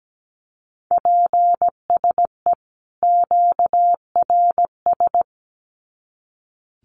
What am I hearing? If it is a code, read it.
Morse code, heard as a keyed tone, about 17 wpm, 700 Hz. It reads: PSE QRS
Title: Common QSO Phrases 17wpm